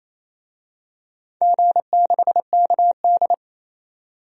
Morse code, heard as a keyed tone, about 28 wpm, 700 Hz. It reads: G6KD
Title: DXCC-100 Call Signs - 4 Characters - 28wpm